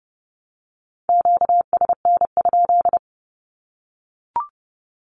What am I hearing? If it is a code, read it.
Morse code, heard as a keyed tone, about 30 wpm, 700 Hz. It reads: QSN?
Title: Q-Codes Repeated 3x 30wpm